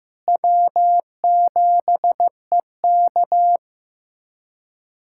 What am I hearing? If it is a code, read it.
Morse code, heard as a keyed tone, about 15 wpm, 700 Hz. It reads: W7EK
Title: USA Callsign DE Callsign - 4 Characters - 15wpm